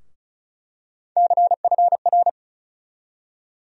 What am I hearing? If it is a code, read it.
Morse code, heard as a keyed tone, about 34 wpm, 700 Hz. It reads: CFR
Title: USA Call Sign Postfixes - 3 Letters 35wpm